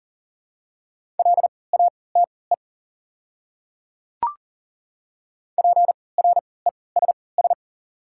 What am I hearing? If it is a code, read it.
Morse code, heard as a keyed tone, about 40 wpm, spaced as for 23 wpm, 700 Hz. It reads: LATE PRESS
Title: 3x Character Spacing - Top 300 Words - 40wpm